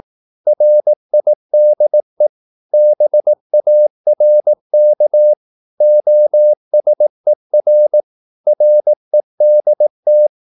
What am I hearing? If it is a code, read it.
Morse code, heard as a keyed tone, about 18 wpm, 600 Hz. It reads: RIDE BARK OSER REDT
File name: Korte ord DK 18wpm